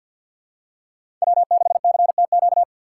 Morse code, heard as a keyed tone, about 51 wpm, 700 Hz. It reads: W6CTQ
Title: USA Call Signs - 5 Characters - 4x Character Spacing - 50wpm